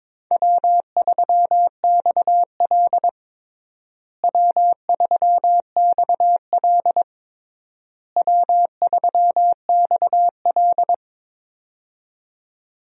Morse code, heard as a keyed tone, about 22 wpm, 700 Hz. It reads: W3XL W3XL W3XL
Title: USA Call Signs - 4 Characters - Repeated 3x - 22wpm